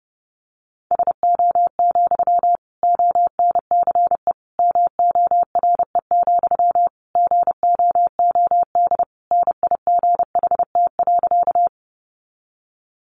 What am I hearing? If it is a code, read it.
Morse code, heard as a keyed tone, about 30 wpm, 700 Hz. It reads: SO, ONCE MORE, GOOD NIGHT.
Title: Sentences from Top 300 Words 30wpm